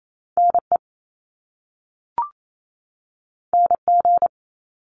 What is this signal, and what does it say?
Morse code, heard as a keyed tone, about 28 wpm, 700 Hz. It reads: NE NG
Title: Top Two Letter Combinations in Words 28wpm